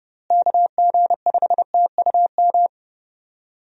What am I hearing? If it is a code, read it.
Morse code, heard as a keyed tone, about 30 wpm, 700 Hz. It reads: KG5TUM
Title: USA Call Signs - 6 Characters - Repeated 3x - 30wpm